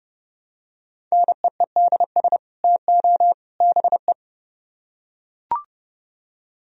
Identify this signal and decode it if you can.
Morse code, heard as a keyed tone, about 30 wpm, 700 Hz. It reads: NEEDS TO BE